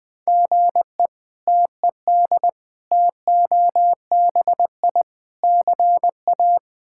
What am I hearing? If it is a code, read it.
Morse code, heard as a keyed tone, about 20 wpm, 700 Hz. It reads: GE TED TOBI CA
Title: K1USN SST QSO Practice - 20wpm